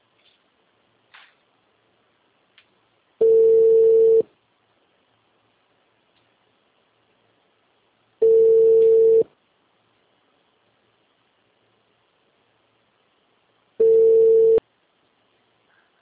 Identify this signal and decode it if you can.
Ringback tone (China, 450 Hz).